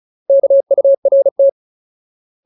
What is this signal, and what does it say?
Morse code, heard as a keyed tone, about 34 wpm, 550 Hz. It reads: KURT
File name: Ultimate Word List / 35WPM Ultimate